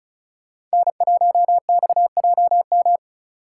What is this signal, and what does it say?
Morse code, heard as a keyed tone, about 36 wpm, 700 Hz. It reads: N1XJM
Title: K1USN SST QSO Practice - 35wpm